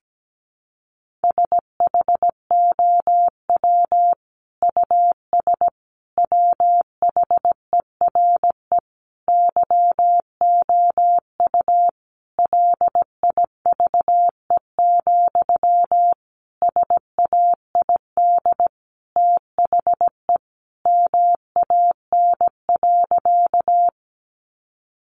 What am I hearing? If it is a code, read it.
Morse code, heard as a keyed tone, about 17 wpm, 700 Hz. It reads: SHOW US WHERE YOU LIVE, SAID THE MAN.
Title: Sentences from Top 200 Words 17wpm